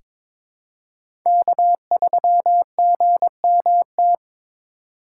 Morse code, heard as a keyed tone, about 21 wpm, 700 Hz. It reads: K3GMT
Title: K1USN SST QSO Practice - 22wpm